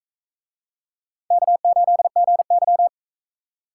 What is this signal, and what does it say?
Morse code, heard as a keyed tone, about 41 wpm, 700 Hz. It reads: K8GY